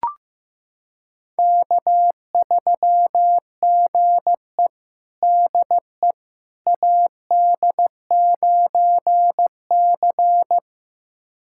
Morse code, heard as a keyed tone, about 16 wpm, 700 Hz. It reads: K3GE DE AD9C
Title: USA Callsign DE Callsign - 4 Characters - 15wpm